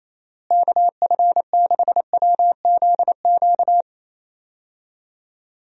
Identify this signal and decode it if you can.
Morse code, heard as a keyed tone, about 28 wpm, 700 Hz. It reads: KF6WZQ
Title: USA Call Signs - 6 Characters - Repeated 3x - 28wpm